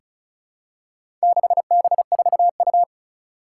Morse code, heard as a keyed tone, about 34 wpm, 700 Hz. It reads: 6B4U
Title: Four Challenging Letter-Number Combinations 35wpm